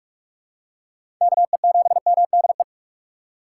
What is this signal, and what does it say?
Morse code, heard as a keyed tone, about 44 wpm, 690 Hz. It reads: KE7MDE
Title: USA Call Signs - 4x Character Spacing - 45wpm